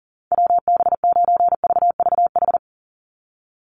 Morse code, heard as a keyed tone, about 37 wpm, 700 Hz. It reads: WB9VVH
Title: USA Call Signs 40wpm